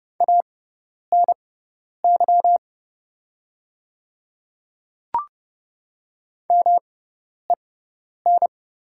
Morse code, heard as a keyed tone, about 30 wpm, spaced as for 10 wpm, 700 Hz. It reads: ANY MEN
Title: 6x Character Spacing - Top 200 Words - 30wpm